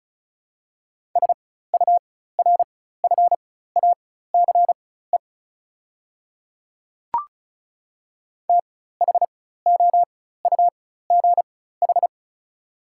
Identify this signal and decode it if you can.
Morse code, heard as a keyed tone, about 36 wpm, spaced as for 16 wpm, 700 Hz. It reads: SURFACE THOUGH
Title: 4x Character Spacing - Top 500 Words - 35wpm